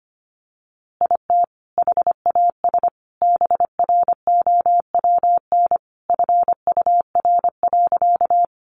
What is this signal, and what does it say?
Morse code, heard as a keyed tone, about 25 wpm, 700 Hz. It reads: IT HAS BROWN FUR.
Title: Easy Phrases 25wpm